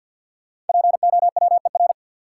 Morse code, heard as a keyed tone, about 52 wpm, 700 Hz. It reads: POWER
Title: Sentences from Top 500 Words - New Word Drill - 50wpm